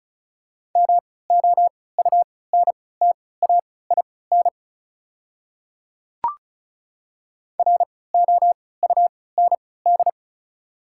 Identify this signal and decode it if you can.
Morse code, heard as a keyed tone, about 36 wpm, spaced as for 20 wpm, 690 Hz. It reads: MOUNTAIN ROUND
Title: 3x Character Spacing - Top 300 Words - 35wpm